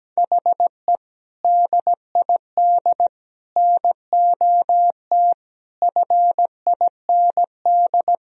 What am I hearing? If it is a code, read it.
Morse code, heard as a keyed tone, about 17 wpm, 700 Hz. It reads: HE DID NOT FIND
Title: Sentences from Top 1000 Words - Review Part 3 - 17wpm